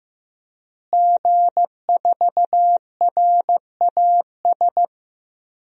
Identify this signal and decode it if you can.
Morse code, heard as a keyed tone, about 15 wpm, 700 Hz. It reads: G4RAS
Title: DXCC-100 Call Signs - 5 Characters - Repeated 2x - 15wpm